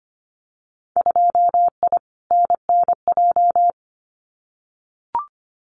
Morse code, heard as a keyed tone, about 25 wpm, 700 Hz. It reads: 2I NNJ
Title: Winter Field Day Exchanges Encore 25wpm